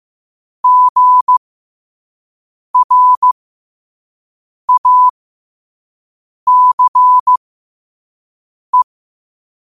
Morse code, heard as a keyed tone, about 15 wpm, spaced as for 5 wpm, 1000 Hz. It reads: GRACE
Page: New quotes every day in morse code at 5 Words per minute.